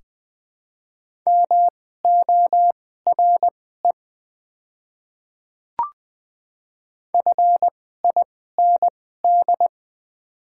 Morse code, heard as a keyed tone, about 20 wpm, spaced as for 14 wpm, 700 Hz. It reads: MORE FIND
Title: 2x Character Spacing - Top 300 Words - 20wpm